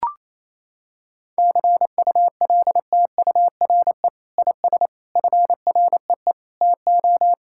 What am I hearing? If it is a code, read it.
Morse code, heard as a keyed tone, about 28 wpm, 700 Hz. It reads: CULTURE IS FREE TO
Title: Sentences from Top 1000 Words - Review Part 6 - 28wpm